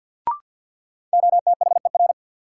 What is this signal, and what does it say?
Morse code, heard as a keyed tone, about 54 wpm, 700 Hz. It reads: OTHER
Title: Sentences from Top 100 Words - New Word Drill - 50wpm